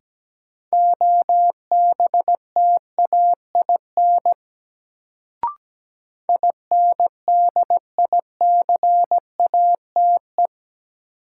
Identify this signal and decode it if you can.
Morse code, heard as a keyed tone, about 17 wpm, 700 Hz. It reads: OBTAIN INDICATE
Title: Sentences from Top 1300 Words - New Word Drill Alt - 17wpm